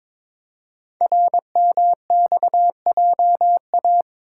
Morse code, heard as a keyed tone, about 22 wpm, 700 Hz. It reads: RMXJA
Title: Code Groups - 5 Letters - 22wpm